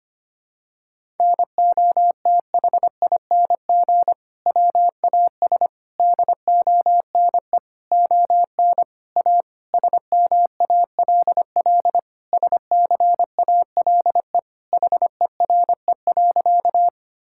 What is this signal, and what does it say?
Morse code, heard as a keyed tone, about 25 wpm, 700 Hz. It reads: NOTHING WAS DONE ON A SMALL SCALE HERE.
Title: Sentences from Top 1000 Words - Review Part 5 - 25wpm